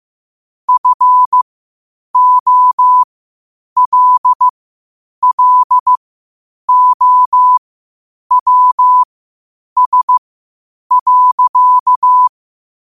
Morse code, heard as a keyed tone, about 15 wpm, spaced as for 8 wpm, 1000 Hz. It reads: FOLLOWS.
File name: qotdfiles / 8wpm